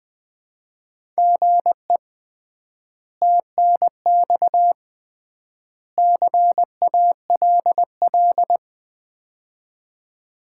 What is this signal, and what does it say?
Morse code, heard as a keyed tone, about 20 wpm, 700 Hz. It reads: GE TNX CALL
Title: Common QSO Phrases - 3x Word-Spacing - 20wpm